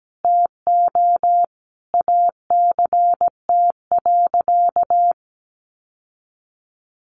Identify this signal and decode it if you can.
Morse code, heard as a keyed tone, about 17 wpm, 700 Hz. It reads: TO ACT.
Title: Sentences from Top 1200 Words 17wpm